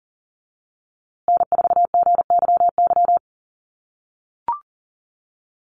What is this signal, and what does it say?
Morse code, heard as a keyed tone, about 40 wpm, 700 Hz. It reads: N4GYY
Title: USA Call Signs Repeated 2x 40wpm